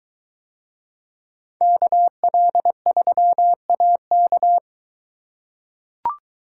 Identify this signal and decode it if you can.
Morse code, heard as a keyed tone, about 22 wpm, 700 Hz. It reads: KL3AK